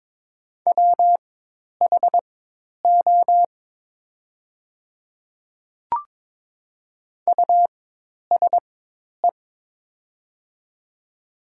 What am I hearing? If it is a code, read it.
Morse code, heard as a keyed tone, about 22 wpm, spaced as for 10 wpm, 700 Hz. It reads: WHO USE
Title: 4x Character Spacing - Top 200 Words - 22wpm